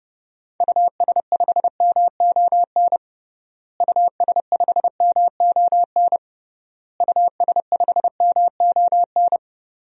Morse code, heard as a keyed tone, about 30 wpm, 700 Hz. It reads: US5MON US5MON US5MON
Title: DXCC-100 Call Signs - 6 Characters - Repeated 3x - 30wpm